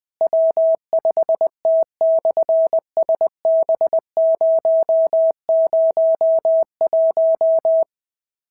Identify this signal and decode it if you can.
Morse code, heard as a keyed tone, about 19 wpm, 650 Hz. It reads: W5T/SB001
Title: SOTA QSO Practice 20wpm